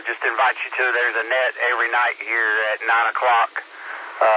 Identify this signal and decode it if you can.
Emergency Alert System (SAME) tones, heard as harsh buzzing bursts, with speech. It is heard from 0.7 to 4.1 s.
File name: SimplexLogic